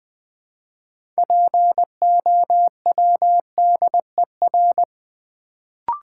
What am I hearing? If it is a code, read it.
Morse code, heard as a keyed tone, about 20 wpm, 700 Hz. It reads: POWDER
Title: Build A Word - 2 Letters At A Time - 6-Letter Words - 20wpm